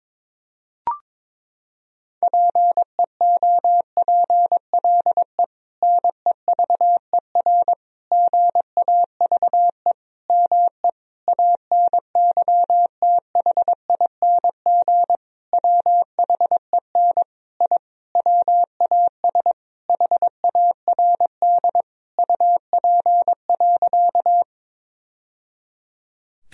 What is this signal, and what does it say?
Morse code, heard as a keyed tone, about 22 wpm, 700 Hz. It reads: PEOPLE NEVER GAVE ME ANYTHING WHEN I WAS HARD UP.
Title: Sentences from Top 1000 Words - Review Part 1 - 22wpm